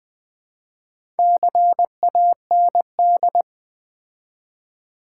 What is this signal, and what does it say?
Morse code, heard as a keyed tone, about 20 wpm, 700 Hz. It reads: CAND
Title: Build A Word - 2 Letters At A Time - 6-Letter Words - 20wpm